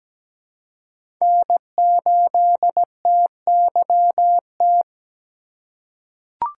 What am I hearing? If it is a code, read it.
Morse code, heard as a keyed tone, about 17 wpm, 700 Hz. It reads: N8TYT
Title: USA Call Signs - 5 Characters - 6x Character Spacing - 17wpm